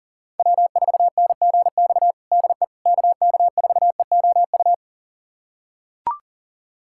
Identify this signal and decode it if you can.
Morse code, heard as a keyed tone, about 40 wpm, 700 Hz. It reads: W4NGX DE KK4EOU
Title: USA Callsign DE Callsign 40wpm